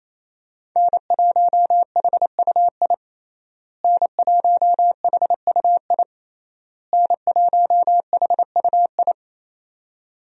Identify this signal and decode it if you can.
Morse code, heard as a keyed tone, about 27 wpm, 700 Hz. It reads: N1HUI N1HUI N1HUI
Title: USA Call Signs - 5 Characters - Repeated 3x - 28wpm